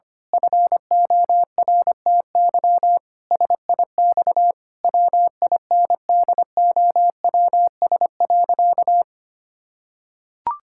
Morse code, heard as a keyed tone, about 25 wpm, 700 Hz. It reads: FORTY SIX WINDOWS.
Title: Pangram Challenge v2 - 25wpm